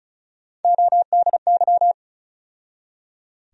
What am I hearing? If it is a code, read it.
Morse code, heard as a keyed tone, about 34 wpm, 700 Hz. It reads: ODY
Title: USA Call Sign Postfixes - 3 Letters 35wpm